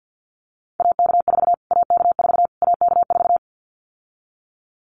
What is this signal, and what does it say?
Morse code, heard as a keyed tone, about 50 wpm, 700 Hz. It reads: AK4 AK4 AK4
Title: Callsign Prefixes for Top 25 DXCC - Repeated 3x - 50wpm